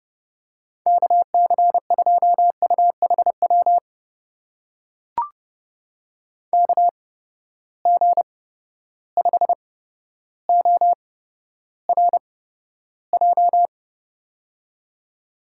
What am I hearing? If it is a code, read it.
Morse code, heard as a keyed tone, about 30 wpm, spaced as for 13 wpm, 700 Hz. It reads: KC2UHW KG5ORJ
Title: USA Call Signs - 6 Characters - 8x Character Spacing - 30wpm